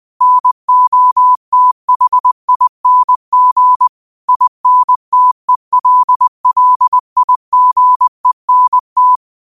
Morse code, heard as a keyed tone, about 20 wpm, 1000 Hz. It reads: NOTHING INTELLIGENT
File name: qotdfiles / 20wpm